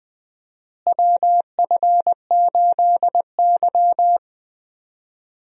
Morse code, heard as a keyed tone, about 20 wpm, 700 Hz. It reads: WF8Y
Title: USA Call Signs - 4 Characters - Repeated 2x - 20wpm